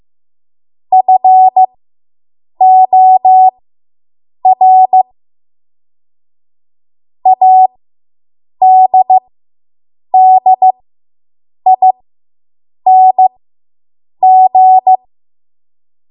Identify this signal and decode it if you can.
Morse code, heard as a keyed tone, about 15 wpm, spaced as for 7 wpm, 750 Hz. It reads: FOR ADDING